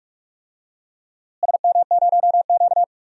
Morse code, heard as a keyed tone, about 43 wpm, 700 Hz. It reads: SM0Q
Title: DXCC-100 Call Signs - 4 Characters - Repeated 2x - 45wpm